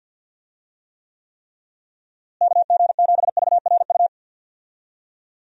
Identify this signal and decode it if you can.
Morse code, heard as a keyed tone, about 52 wpm, 700 Hz. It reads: KG7VRU
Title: USA Call Signs - 6 Characters - Repeated 3x - 50wpm